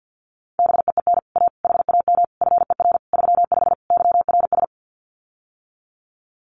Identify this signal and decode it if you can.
Morse code, heard as a keyed tone, about 48 wpm, 700 Hz. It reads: BEEN A HAM FER 35 YRS
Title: Common QSO Phrases Encore 50wpm